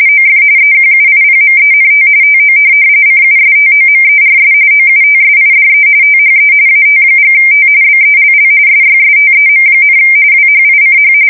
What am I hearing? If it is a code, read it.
RTTY, about 45.45 baud. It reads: OF EQUIPMENT TO CUSTOMERS OUTSIDE THE AT&T BELL SYSTEM, WHICH
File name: fsk170 / teletype-corp